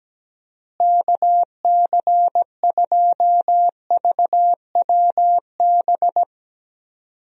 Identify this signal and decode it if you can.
Morse code, heard as a keyed tone, about 17 wpm, 700 Hz. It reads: KC2VWB
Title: USA Call Signs - 6 Characters - Repeated 3x - 17wpm